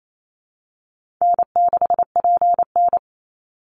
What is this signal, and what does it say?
Morse code, heard as a keyed tone, about 28 wpm, 700 Hz. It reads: N6PN
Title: Easy Call Signs Repeated 2x - 28wpm